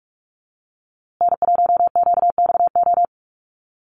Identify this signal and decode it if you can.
Morse code, heard as a keyed tone, about 44 wpm, 700 Hz. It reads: N1QXO
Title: USA Call Signs Repeated 2x 45wpm